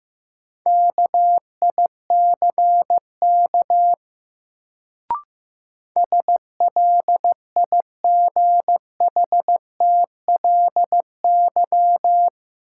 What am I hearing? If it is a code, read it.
Morse code, heard as a keyed tone, about 15 wpm, 700 Hz. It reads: KICK SLIGHTLY
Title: Sentences from Top 1300 Words - New Word Drill - 15wpm